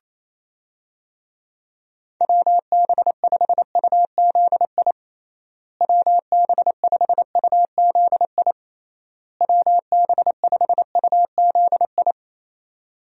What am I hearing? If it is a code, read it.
Morse code, heard as a keyed tone, about 27 wpm, 700 Hz. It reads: WB5UZI WB5UZI WB5UZI
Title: USA Call Signs - 6 Characters - Repeated 3x - 28wpm